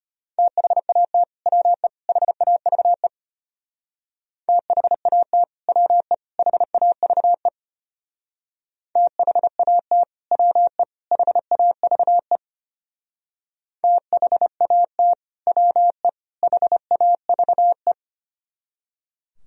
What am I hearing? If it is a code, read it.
Morse code, heard as a keyed tone, about 31 wpm, 700 Hz. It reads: THAT WE HAVE THAT WE HAVE THAT WE HAVE THAT WE HAVE
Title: Speed-Racing - Sets of 3 Words 25wpm